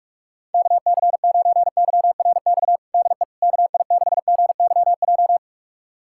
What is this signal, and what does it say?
Morse code, heard as a keyed tone, about 46 wpm, 690 Hz. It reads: KC9YRX DE KI6GYJ
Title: USA Callsign DE Callsign 45wpm